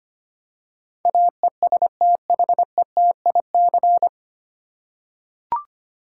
Morse code, heard as a keyed tone, about 25 wpm, 700 Hz. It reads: AESTHETIC
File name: practice / 25wpm / 9 Letter Words - First 3 Letters Given - 25wpm